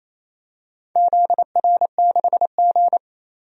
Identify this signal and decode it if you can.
Morse code, heard as a keyed tone, about 28 wpm, 700 Hz. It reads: ZR6G
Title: DXCC-100 Call Signs - 4 Characters - 28wpm